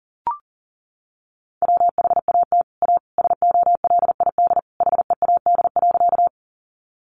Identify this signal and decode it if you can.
Morse code, heard as a keyed tone, about 40 wpm, 700 Hz. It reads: WHAT A SOLID HEAD.
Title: Sentences from Top 1500 Words 40wpm